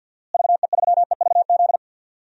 Morse code, heard as a keyed tone, about 48 wpm, 700 Hz. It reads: VE3EVZ
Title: DXCC-100 Call Signs - 6 Characters - Repeated 3x - 50wpm